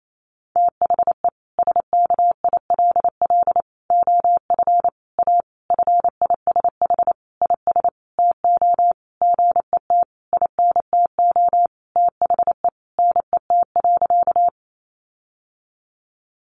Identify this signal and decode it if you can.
Morse code, heard as a keyed tone, about 28 wpm, 700 Hz. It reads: THE SKILL OF A FISH IS TO GET INTO THE NET.
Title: Sentences from Top 1400 Words 28wpm